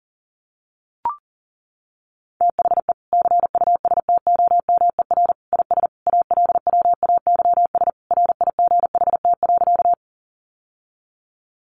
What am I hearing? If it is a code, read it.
Morse code, heard as a keyed tone, about 40 wpm, 700 Hz. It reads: THE CUSTOMER IS ALWAYS RIGHT.
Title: Sentences from Top 1100 Words 40wpm